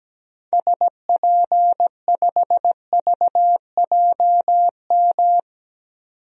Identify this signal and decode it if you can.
Morse code, heard as a keyed tone, about 17 wpm, 700 Hz. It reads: SP5VJM
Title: DXCC-100 Call Signs - 6 Characters - 17wpm